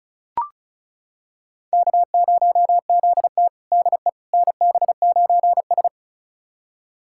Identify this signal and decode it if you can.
Morse code, heard as a keyed tone, about 36 wpm, 700 Hz. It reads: K0ZT DE NB9S
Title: USA Callsign DE Callsign - 4 Characters - 35wpm